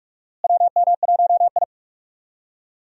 Morse code, heard as a keyed tone, about 43 wpm, 700 Hz. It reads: WM1I
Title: USA Call Signs - 4 Characters - Repeated 3x - 45wpm